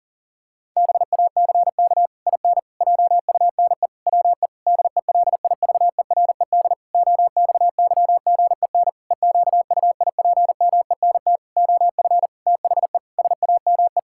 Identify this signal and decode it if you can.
Morse code, heard as a keyed tone, about 40 wpm, 700 Hz. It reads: BACK IN JUNE WE DELIVERED OXYGEN EQUIPMENT OF THE SAME
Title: Pangram Challenge v2 - 40wpm